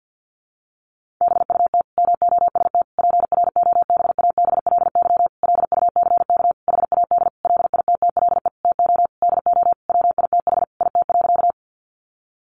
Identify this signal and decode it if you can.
Morse code, heard as a keyed tone, about 50 wpm, 700 Hz. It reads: BUT MOST PROBABLY LUCK HAD LITTLE TO DO WITH IT.
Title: Sentences from Top 1500 Words 50wpm